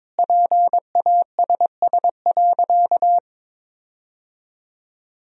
Morse code, heard as a keyed tone, about 22 wpm, 690 Hz. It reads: PASS.
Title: 4x Word-Spacing - Sentences from Top 400 Words - 22wpm